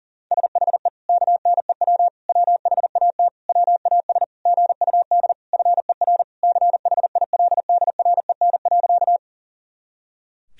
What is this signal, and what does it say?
Morse code, heard as a keyed tone, about 37 wpm, 700 Hz. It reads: SHE KNEW WHAT WAS GUD FER CHILDREN.
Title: Sentences from Top 400 Words and Abbreviations 40wpm